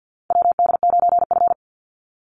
Morse code, heard as a keyed tone, about 50 wpm, 700 Hz. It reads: WD9F
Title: NA SOTA Chaser Callsigns 50wpm